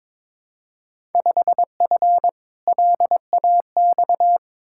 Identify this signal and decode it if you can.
Morse code, heard as a keyed tone, about 22 wpm, 700 Hz. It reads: 5F LAX
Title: Field Day Exchanges Encore - 22wpm